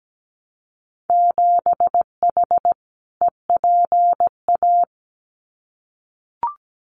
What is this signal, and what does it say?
Morse code, heard as a keyed tone, about 17 wpm, 700 Hz. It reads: 7H EPA
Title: Winter Field Day Exchanges Encore 17wpm